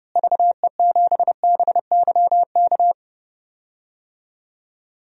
Morse code, heard as a keyed tone, about 32 wpm, 700 Hz. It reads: VE7BYK
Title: DXCC-100 Call Signs - 6 Characters - Repeated 2x - 30wpm